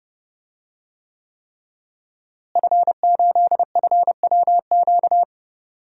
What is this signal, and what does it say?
Morse code, heard as a keyed tone, about 30 wpm, 700 Hz. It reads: F8FWQ
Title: DXCC-100 Call Signs - 5 Characters - Repeated 3x - 30wpm